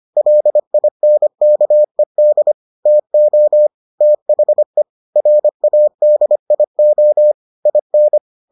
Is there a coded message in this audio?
Yes: Morse code from 0.2 to 8.2 s, heard as a keyed tone, about 25 wpm, 600 Hz. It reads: LINKED TO THE RADIO IN